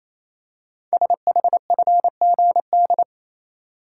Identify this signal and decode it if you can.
Morse code, heard as a keyed tone, about 29 wpm, 700 Hz. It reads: SHFGD
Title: Code Groups - 5 Letters - 28wpm